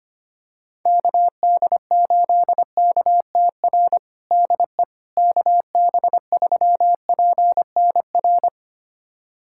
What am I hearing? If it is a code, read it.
Morse code, heard as a keyed tone, about 25 wpm, 700 Hz. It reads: KD8KTR DE KB3PNR
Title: USA Callsign DE Callsign - 6 Characters - 25wpm